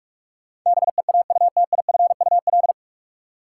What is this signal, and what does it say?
Morse code, heard as a keyed tone, about 46 wpm, 700 Hz. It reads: BEAUTIFUL
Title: Sentences from Top 900 Words - New Word Drill - 45wpm